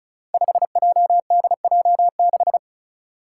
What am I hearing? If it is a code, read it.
Morse code, heard as a keyed tone, about 36 wpm, 700 Hz. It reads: 5JDJ6
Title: Five Challenging Letter-Number Combinations 35wpm